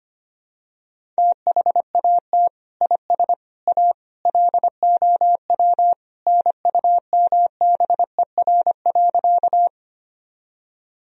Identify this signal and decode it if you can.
Morse code, heard as a keyed tone, about 25 wpm, 700 Hz. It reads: THAT IS A LOW NUMBER.
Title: Sentences from Top 1000 Words - Review Part 1 - 25wpm